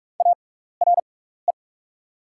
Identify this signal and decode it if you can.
Morse code, heard as a keyed tone, about 43 wpm, spaced as for 15 wpm, 700 Hz. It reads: ARE
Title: 6x Character Spacing - Top 200 Words - 45wpm